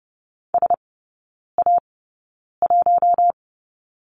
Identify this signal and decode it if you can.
Morse code, heard as a keyed tone, about 30 wpm, spaced as for 9 wpm, 700 Hz. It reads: SA1
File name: 081 Lesson 14e - Number 2 - Three Characters 30wpm